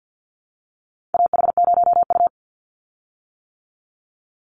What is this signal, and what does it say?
Morse code, heard as a keyed tone, about 50 wpm, 700 Hz. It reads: AH0U
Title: Call Signs - 50wpm